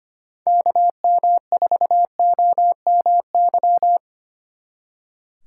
Morse code, heard as a keyed tone, about 33 wpm, 700 Hz. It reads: KM4OMY